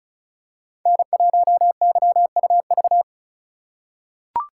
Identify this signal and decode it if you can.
Morse code, heard as a keyed tone, about 36 wpm, 700 Hz. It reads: N1YUV
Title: USA Call Signs - 4x Character Spacing - 35wpm